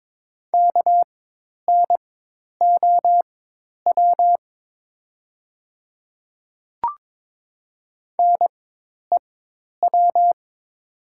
Morse code, heard as a keyed tone, about 22 wpm, spaced as for 10 wpm, 700 Hz. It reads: KNOW NEW
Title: 4x Character Spacing - Top 300 Words - 22wpm